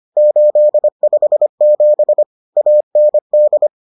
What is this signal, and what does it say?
Morse code, heard as a keyed tone, about 25 wpm, 600 Hz. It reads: 857 AND